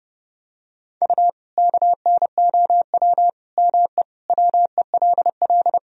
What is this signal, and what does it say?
Morse code, heard as a keyed tone, about 32 wpm, 700 Hz. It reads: U KNOW ME WELL
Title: Sentences from Top 400 Words and Abbreviations 30wpm